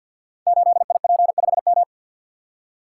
Morse code, heard as a keyed tone, about 52 wpm, 700 Hz. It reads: 8IP5M
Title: Code Groups - 5 Letter:Numbers - 50wpm